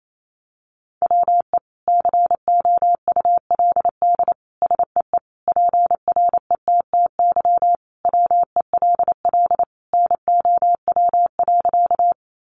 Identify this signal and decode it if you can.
Morse code, heard as a keyed tone, about 28 wpm, 700 Hz. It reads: WE COULD SEE PRETTY WELL NOW.
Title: Sentences from Top 1000 Words 28wpm